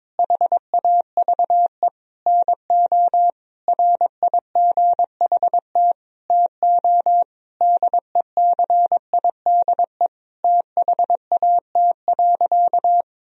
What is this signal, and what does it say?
Morse code, heard as a keyed tone, about 22 wpm, 700 Hz. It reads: HAVE NO RIGHT TO DECIDE THAT.
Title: Sentences from Top 1000 Words - Review Part 3 - 22wpm